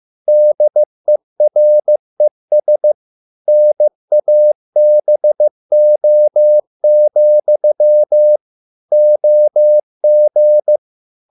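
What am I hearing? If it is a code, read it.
Morse code, heard as a keyed tone, about 15 wpm, 600 Hz. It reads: DERES NABO, OG
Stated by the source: Morse code at a slow 15 wpm from CW med Gnister.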